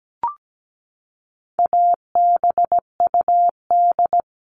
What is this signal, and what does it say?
Morse code, heard as a keyed tone, about 17 wpm, 700 Hz. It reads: ABUD
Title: All Cut Numbers - 4-Digit 17wpm